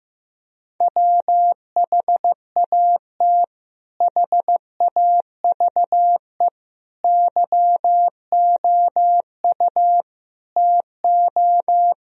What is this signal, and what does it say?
Morse code, heard as a keyed tone, about 15 wpm, 700 Hz. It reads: WHAT HAVE YOU TO
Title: Sentences from Top 1000 Words - Review Part 3 - 15wpm